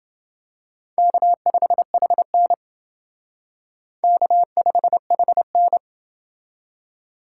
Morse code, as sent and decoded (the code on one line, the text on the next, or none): -.- ..... .... -. / -.- ..... .... -.
K5HN K5HN